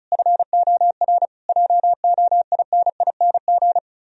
Morse code, heard as a keyed tone, about 46 wpm, 700 Hz. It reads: FOR JOINING